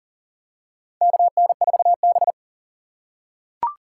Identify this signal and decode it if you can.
Morse code, heard as a keyed tone, about 40 wpm, 700 Hz. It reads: KN4B
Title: USA Call Signs - 3x Character Spacing - 40wpm